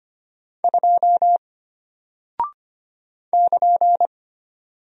Morse code, heard as a keyed tone, about 25 wpm, 700 Hz. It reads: 2 <KN>
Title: Single Character (Letter:Number:Punctuation:Prosign) Alt 25wpm